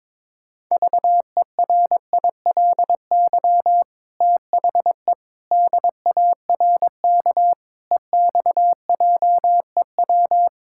Morse code, heard as a keyed tone, about 22 wpm, 700 Hz. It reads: VERILY THE DARK EXJEW
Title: Pangram Challenge v2 - 22wpm